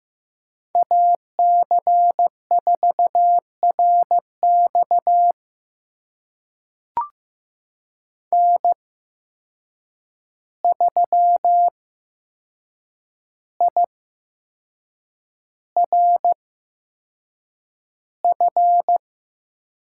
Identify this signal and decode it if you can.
Morse code, heard as a keyed tone, about 15 wpm, spaced as for 6 wpm, 700 Hz. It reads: AC4RX N3IRF
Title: USA Call Signs - 8x Character Spacing - 15wpm